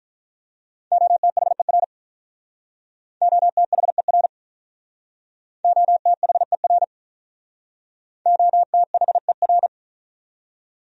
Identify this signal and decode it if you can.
Morse code, heard as a keyed tone, about 46 wpm, 700 Hz. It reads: OTHER OTHER OTHER OTHER